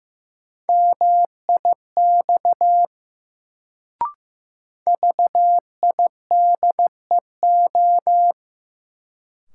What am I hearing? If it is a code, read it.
Morse code, heard as a keyed tone, about 15 wpm, 700 Hz. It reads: MIX VIDEO
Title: Sentences from Top 1000 Words - Review Part 4 - New Word Drill - 15wpm